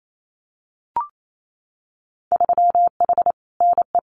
Morse code, heard as a keyed tone, about 28 wpm, 700 Hz. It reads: 3H NE
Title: Winter Field Day Exchanges Encore 28wpm